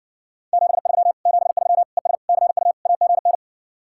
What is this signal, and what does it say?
Morse code, heard as a keyed tone, about 60 wpm, 700 Hz. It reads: 73 73 ES CU AGN